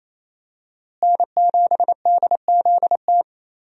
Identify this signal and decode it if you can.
Morse code, heard as a keyed tone, about 28 wpm, 700 Hz. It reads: N7DZT
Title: USA Call Signs - 5 Characters - Repeated 2x - 28wpm